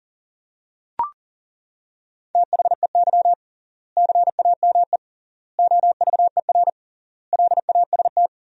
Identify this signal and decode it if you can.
Morse code, heard as a keyed tone, about 40 wpm, 700 Hz. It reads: THEY CAME OVER LAST
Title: 3x Word-Spacing - Sentences from Top 300 Words - 40wpm